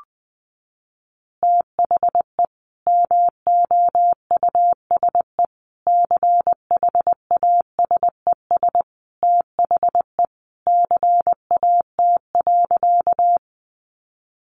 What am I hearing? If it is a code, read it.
Morse code, heard as a keyed tone, about 20 wpm, 700 Hz. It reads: THE MOUSE CHASES THE CAT.
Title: Easy Phrases 20wpm